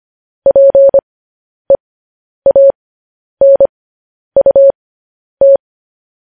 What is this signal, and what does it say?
Morse code, heard as a keyed tone, about 25 wpm, spaced as for 10 wpm, 550 Hz. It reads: PEANUT
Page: news headlines in morse code at 10 wpm